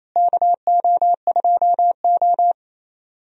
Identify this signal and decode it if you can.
Morse code, heard as a keyed tone, about 28 wpm, 700 Hz. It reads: KO2O
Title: USA Call Signs - 4 Characters - Repeated 2x - 28wpm